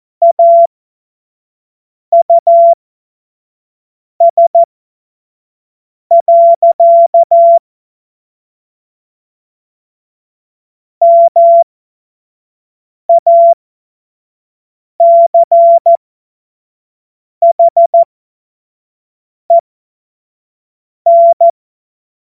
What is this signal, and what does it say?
Morse code, heard as a keyed tone, about 14 wpm, spaced as for 5 wpm, 680 Hz. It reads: AUS. MACHEN
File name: Morsekurs / Witze / Witz 13 Mai 2025 25 F70